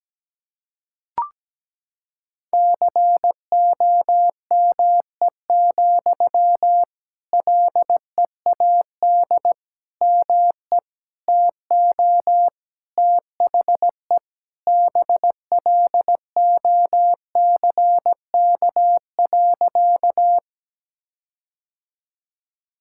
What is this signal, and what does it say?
Morse code, heard as a keyed tone, about 17 wpm, 700 Hz. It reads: COME, LEAD ME TO THE BLOCK.
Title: Sentences from Top 1600 Words 17wpm